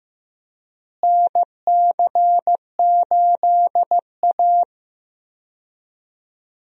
Morse code, heard as a keyed tone, about 15 wpm, 700 Hz. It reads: NC8A
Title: USA Call Signs - 4 Characters - Repeated 3x - 15wpm